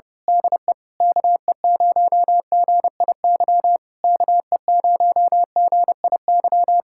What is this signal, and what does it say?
Morse code, heard as a keyed tone, about 24 wpm, 700 Hz. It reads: DE KE0GIY KE0GIY
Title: Speed-Racing Common QSO Phrases Encore - 25wpm